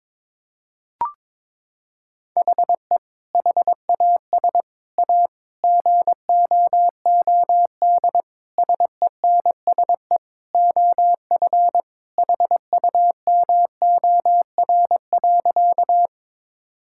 Morse code, heard as a keyed tone, about 23 wpm, 700 Hz. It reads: HE HAS A GOOD SENSE OF HUMOR.
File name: practice / 22wpm / Sentences with 2-Word Phrases - 22wpm